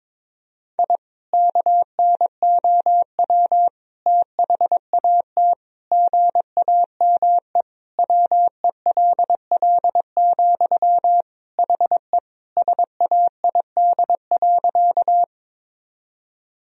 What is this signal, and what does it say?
Morse code, heard as a keyed tone, about 22 wpm, 700 Hz. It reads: I KNOW THAT GAME WELL, HE SAID.
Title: Sentences from Top 1000 Words - Review Part 3 - 22wpm